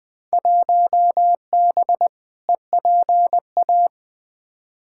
Morse code, heard as a keyed tone, about 20 wpm, 700 Hz. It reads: 1B EPA
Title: Field Day Exchanges Encore - 20wpm